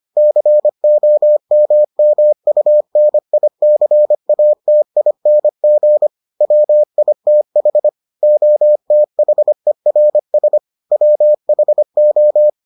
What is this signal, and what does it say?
Morse code, heard as a keyed tone, about 25 wpm, 600 Hz. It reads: COMMUNICATING WITH OTHERS WHO